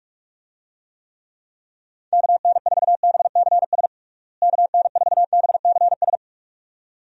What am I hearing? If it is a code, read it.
Morse code, heard as a keyed tone, about 42 wpm, 700 Hz. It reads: KN4BCS KN4BCS
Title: USA Call Signs - 6 Characters - Repeated 2x - 45wpm